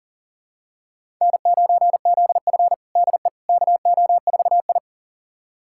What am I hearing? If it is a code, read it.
Morse code, heard as a keyed tone, about 40 wpm, 700 Hz. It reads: N9ZF DE KO4I
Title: USA Callsign DE Callsign - 4 Characters - 40wpm